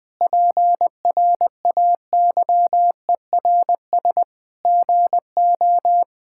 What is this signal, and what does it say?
Morse code, heard as a keyed tone, about 20 wpm, 700 Hz. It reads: PRAYERS GO
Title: Sentences with 8-Word Phrases - 20wpm